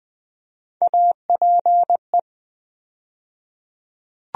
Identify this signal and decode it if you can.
Morse code, heard as a keyed tone, about 20 wpm, 700 Hz. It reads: APE
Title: USA Call Sign Postfixes - 3 Letters 20wpm